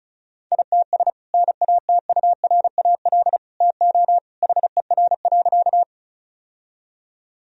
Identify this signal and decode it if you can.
Morse code, heard as a keyed tone, about 34 wpm, 700 Hz. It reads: ITS NATURAL TO HER.
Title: Sentences from Top 1000 Words - Review Part 4 - 35wpm